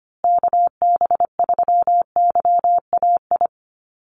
Morse code, heard as a keyed tone, about 25 wpm, 700 Hz. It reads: KB3YAI
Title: USA Call Signs Repeated 2x 25wpm